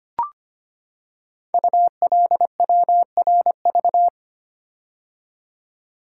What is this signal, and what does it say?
Morse code, heard as a keyed tone, about 25 wpm, 700 Hz. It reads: ULWRV
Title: Code Groups - 5 Letters - 25wpm